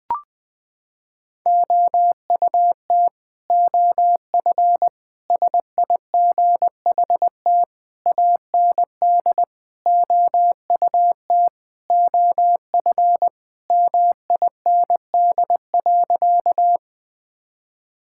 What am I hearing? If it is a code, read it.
Morse code, heard as a keyed tone, about 21 wpm, 700 Hz. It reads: OUT OF SIGHT AND OUT OF MIND.
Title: Sentences with 7-Word Phrases - 20wpm